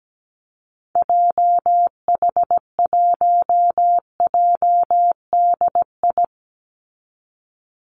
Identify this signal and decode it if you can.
Morse code, heard as a keyed tone, about 17 wpm, 700 Hz. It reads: JH1JDI
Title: Call Signs Repeated 2x - 17wpm